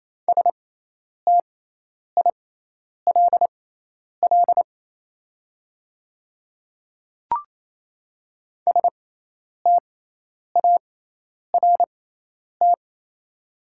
Morse code, heard as a keyed tone, about 27 wpm, spaced as for 10 wpm, 700 Hz. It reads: STILL START
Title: 6x Character Spacing - Top 300 Words - 28wpm